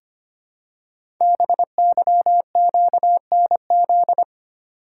Morse code, heard as a keyed tone, about 25 wpm, 700 Hz. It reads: BYQNZ
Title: Code Groups - 5 Letters - 25wpm